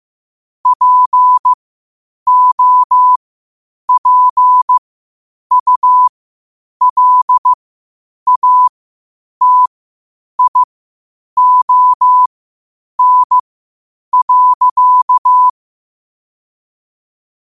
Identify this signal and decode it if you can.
Morse code, heard as a keyed tone, about 15 wpm, spaced as for 8 wpm, 1000 Hz. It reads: POPULATION.
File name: qotdfiles / 8wpm